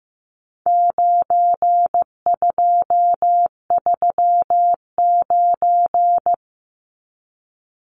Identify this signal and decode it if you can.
Morse code, heard as a keyed tone, about 15 wpm, 700 Hz. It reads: 9239
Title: Numbers - 4-Digit 15wpm